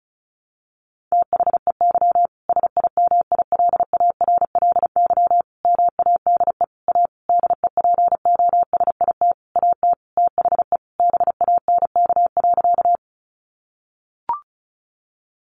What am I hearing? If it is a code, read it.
Morse code, heard as a keyed tone, about 34 wpm, 700 Hz. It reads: THEY SIMILARLY MADE A DEPOSIT AT THE BANK.
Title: Sentences from Top 1400 Words 35wpm